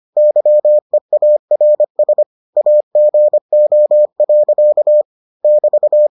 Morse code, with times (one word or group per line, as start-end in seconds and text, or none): 0.2-2.2 YEARS
2.6-5.0 AGO.
5.4-6.1 <BT>